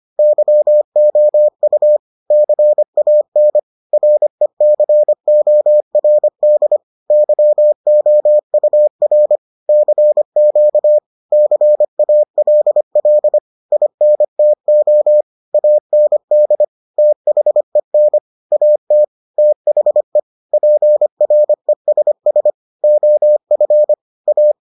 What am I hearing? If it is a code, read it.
Morse code, heard as a keyed tone, about 25 wpm, 600 Hz. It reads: YOU CAN RECORD YOUR CQ CALL INTO AND THEN AT THE PRESS OF A